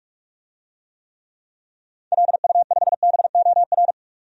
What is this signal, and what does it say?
Morse code, heard as a keyed tone, about 46 wpm, 700 Hz. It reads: LU5BOR
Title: DXCC-100 Call Signs - 6 Characters - 45wpm